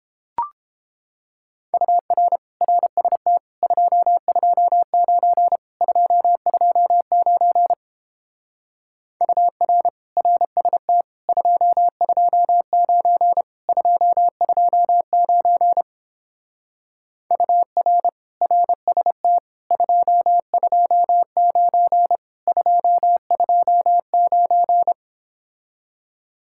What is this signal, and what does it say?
Morse code, heard as a keyed tone, about 30 wpm, 700 Hz. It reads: UR RST 229 229 UR RST 229 229 UR RST 229 229